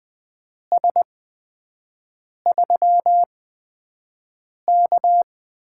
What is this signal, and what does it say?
Morse code, heard as a keyed tone, about 20 wpm, spaced as for 5 wpm, 700 Hz. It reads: S3K
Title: USA Call Signs - 4 Characters - 8x Character Spacing - 20wpm